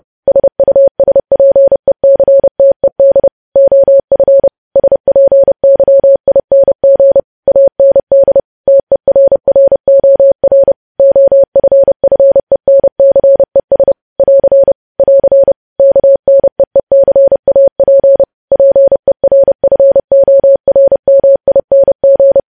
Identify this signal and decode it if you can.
Morse code, heard as a keyed tone, about 30 wpm, 550 Hz. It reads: SUSPECTED OF SPYING AND TERROR OFFENCES <AR> <AR> KNEECAP PERFORMING
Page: news headlines in morse code at 30 wpm